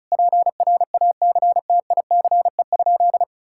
Morse code, heard as a keyed tone, about 36 wpm, 700 Hz. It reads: PRACTICE?
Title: Sentences from Top 1000 Words - Review Part 5 - 35wpm